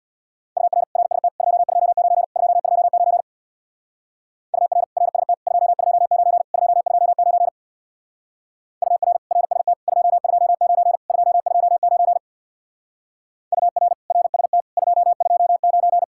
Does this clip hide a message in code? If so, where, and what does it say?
Yes: Morse code from 0.6 to 16.0 s, heard as a keyed tone, about 65 wpm, 700 Hz. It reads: UR RST 229 229 UR RST 229 229 UR RST 229 229 UR RST 229